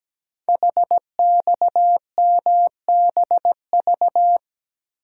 Morse code, heard as a keyed tone, about 17 wpm, 700 Hz. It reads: HXMBV
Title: Code Groups - 5 Letters - 17wpm